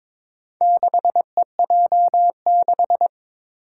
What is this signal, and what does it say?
Morse code, heard as a keyed tone, about 23 wpm, 700 Hz. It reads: 6EJ6
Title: Four Challenging Letter-Number Combinations 22wpm